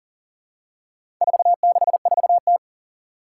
Morse code, heard as a keyed tone, about 40 wpm, 700 Hz. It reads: 464T
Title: Four Challenging Letter-Number Combinations 40wpm